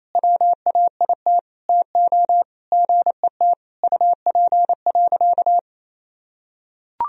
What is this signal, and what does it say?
Morse code, heard as a keyed tone, about 28 wpm, 700 Hz. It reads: WAIT TO GET UP.
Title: Sentences from Top 500 Words and Abbreviations 28wpm